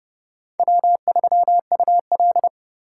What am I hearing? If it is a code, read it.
Morse code, heard as a keyed tone, about 29 wpm, 700 Hz. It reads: W3UL